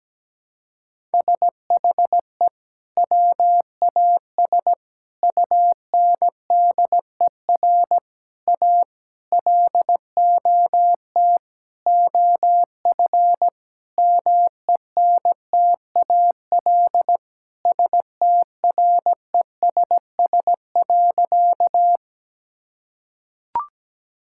Morse code, heard as a keyed tone, about 17 wpm, 700 Hz. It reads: SHE WAS UNDER A LOT OF MENTAL STRESS.
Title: Sentences from Top 1000 Words - Review Part 6 - 17wpm